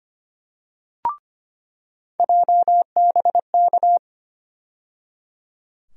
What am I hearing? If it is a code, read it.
Morse code, heard as a keyed tone, about 25 wpm, 700 Hz. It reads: JBK
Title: USA Call Sign Postfixes - 3 Letters 25wpm